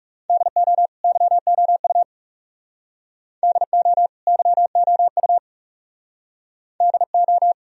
Morse code, heard as a keyed tone, about 41 wpm, 700 Hz. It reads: DO YOU DO YOU DO